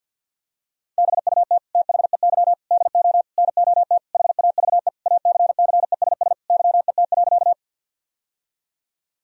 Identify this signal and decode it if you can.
Morse code, heard as a keyed tone, about 50 wpm, 700 Hz. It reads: BUT THEY DO NOT HAVE ACCESS YET.
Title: Sentences from Top 1000 Words - Review Part 4 - 50wpm